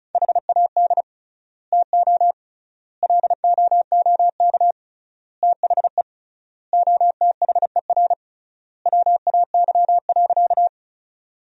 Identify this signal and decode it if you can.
Morse code, heard as a keyed tone, about 36 wpm, 700 Hz. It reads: HAD TO LOOK THE OTHER WAY.
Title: 3x Word-Spacing - Sentences from Top 100 Words - 35wpm